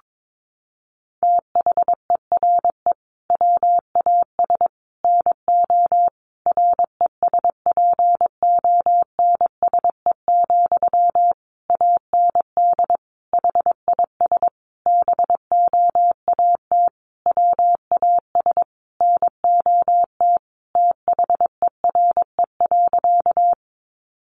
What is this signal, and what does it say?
Morse code, heard as a keyed tone, about 22 wpm, 700 Hz. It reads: THERE WAS NO RESPONSE, AND HIS BOAT WAS NOT THERE.
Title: Sentences from Top 1400 Words 22wpm